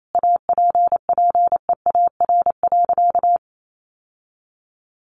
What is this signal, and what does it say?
Morse code, heard as a keyed tone, about 28 wpm, 700 Hz. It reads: APPEAR.
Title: Sentences from Top 1200 Words 28wpm